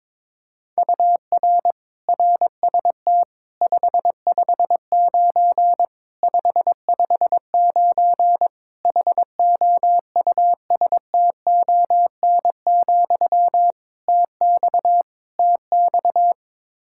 Morse code, heard as a keyed tone, about 22 wpm, 700 Hz. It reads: UR RST 559 559 HOUSTON, TX TX
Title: K1USN SST QSO Practice 22wpm